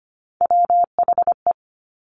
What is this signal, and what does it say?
Morse code, heard as a keyed tone, about 25 wpm, 700 Hz. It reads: WHE
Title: Top Three Letter Combinations in Words 25wpm